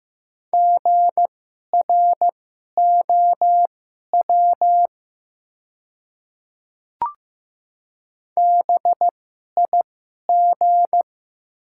Morse code, heard as a keyed tone, about 15 wpm, spaced as for 11 wpm, 700 Hz. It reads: GROW BIG
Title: 2x Character Spacing - Top 300 Words - 15wpm